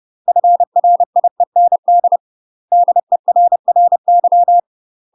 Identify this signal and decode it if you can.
Morse code, heard as a keyed tone, about 30 wpm, 700 Hz. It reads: FRIEND DERRY
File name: UK MASTER QSOs TRAINING / 30WPM / UK MASTER QSOs 0016